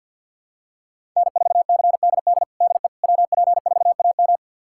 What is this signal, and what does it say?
Morse code, heard as a keyed tone, about 51 wpm, 700 Hz. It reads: N4CDD DE WP4AM
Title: USA Callsign DE Callsign - 5 Characters - 50wpm